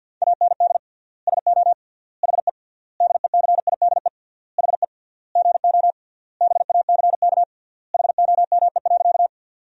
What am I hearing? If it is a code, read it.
Morse code, heard as a keyed tone, about 53 wpm, 700 Hz. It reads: AND SO HE DECIDE HE GO BACK HOME.
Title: 3x Word-Spacing - Sentences from Top 500 Words - 50wpm